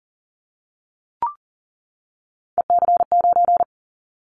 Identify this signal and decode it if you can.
Morse code, heard as a keyed tone, about 39 wpm, 700 Hz. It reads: EC9
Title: Callsign Prefixes for Top 100 DXCC - 40wpm